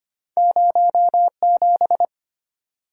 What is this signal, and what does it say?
Morse code, heard as a keyed tone, about 25 wpm, 700 Hz. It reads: 07
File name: Two Letter-Number Combinations - 25wpm